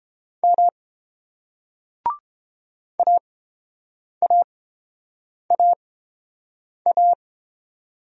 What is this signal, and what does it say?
Morse code, heard as a keyed tone, about 29 wpm, spaced as for 7 wpm, 690 Hz. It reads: M AAAA